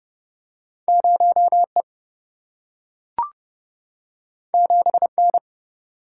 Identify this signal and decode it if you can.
Morse code, heard as a keyed tone, about 31 wpm, 700 Hz. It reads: 0E 7N
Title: Single-Letter Single-Number 30wpm